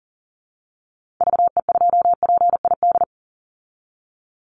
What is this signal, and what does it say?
Morse code, heard as a keyed tone, about 40 wpm, 700 Hz. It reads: VE2PID
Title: Call Signs Repeated 3x - 40wpm